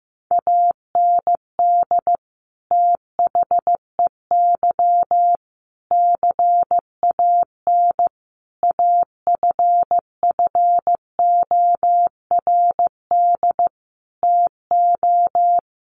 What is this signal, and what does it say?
Morse code, heard as a keyed tone, about 15 wpm, 700 Hz. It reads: AND THEY CAN AFFORD TO
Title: Sentences from Top 1100 Words 15wpm